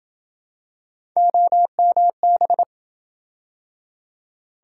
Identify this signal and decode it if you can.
Morse code, heard as a keyed tone, about 28 wpm, 700 Hz. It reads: OMB